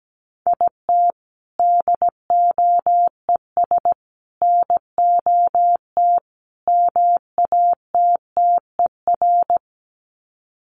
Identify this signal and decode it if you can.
Morse code, heard as a keyed tone, about 17 wpm, 700 Hz. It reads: IT DOES NOT MATTER
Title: Sets of 4 Words 17wpm